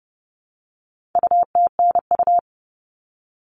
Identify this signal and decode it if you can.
Morse code, heard as a keyed tone, about 30 wpm, 700 Hz. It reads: UTNU
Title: All Cut Numbers - 4-Digit 30wpm